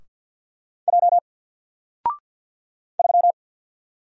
Morse code, heard as a keyed tone, about 48 wpm, 700 Hz. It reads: J 3
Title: Single Letter-Number Alt 50wpm